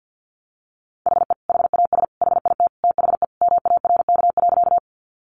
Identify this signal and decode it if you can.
Morse code, heard as a keyed tone, about 50 wpm, 700 Hz. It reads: HE HAS HIT THE MARK.
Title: Sentences from Top 1200 Words 50wpm